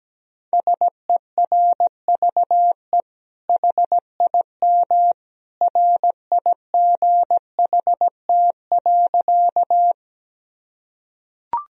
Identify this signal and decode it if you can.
Morse code, heard as a keyed tone, about 17 wpm, 700 Hz. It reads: SERVE HIM RIGHT.
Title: Sentences from Top 1000 Words - Review Part 2 - 17wpm